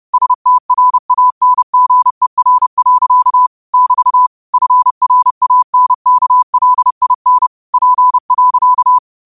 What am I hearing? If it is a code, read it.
Morse code, heard as a keyed tone, about 30 wpm, 1000 Hz. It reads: STRANGER. <BT> FRANKLIN P.
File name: qotdfiles / 30wpm